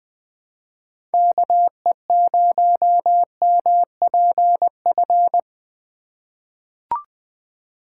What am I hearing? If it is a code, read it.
Morse code, heard as a keyed tone, about 20 wpm, 700 Hz. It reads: KE0MPF
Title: USA Call Signs - 6 Characters - 2x Character Spacing - 20wpm